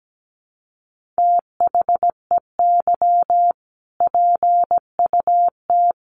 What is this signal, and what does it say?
Morse code, heard as a keyed tone, about 17 wpm, 700 Hz. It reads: THEY PUT
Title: Sentences from Top 1400 Words 17wpm